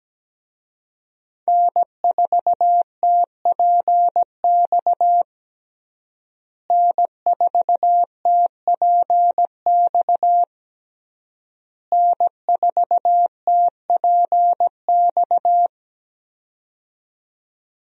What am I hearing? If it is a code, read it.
Morse code, heard as a keyed tone, about 17 wpm, 700 Hz. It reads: N4TPX N4TPX N4TPX
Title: USA Call Signs - 5 Characters - Repeated 3x - 17wpm